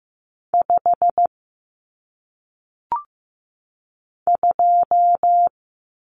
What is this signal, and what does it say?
Morse code, heard as a keyed tone, about 15 wpm, 700 Hz. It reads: 5 2
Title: Numbers - 1-Digit 15wpm